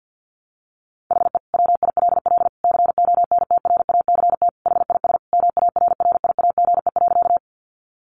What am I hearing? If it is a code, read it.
Morse code, heard as a keyed tone, about 50 wpm, 700 Hz. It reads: HE WILL CONTRACT HIS MARRIAGE.
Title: Sentences from Top 1400 Words 50wpm